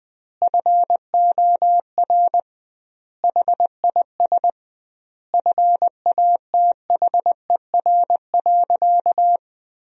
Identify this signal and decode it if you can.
Morse code, heard as a keyed tone, about 21 wpm, 700 Hz. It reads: FOR HIS FATHER.
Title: 2x Word-Spacing - Sentences from Top 500 Words - 20wpm